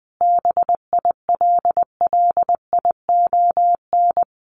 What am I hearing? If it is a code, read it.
Morse code, heard as a keyed tone, about 20 wpm, 700 Hz. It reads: BILLION
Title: A Sound Of Thunder 20wpm